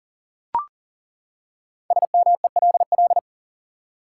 Morse code, heard as a keyed tone, about 40 wpm, 700 Hz. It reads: SMELL
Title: Five Letter Words - 40wpm